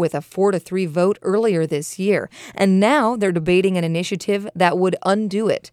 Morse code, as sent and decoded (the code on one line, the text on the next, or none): none